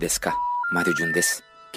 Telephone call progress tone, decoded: special information tone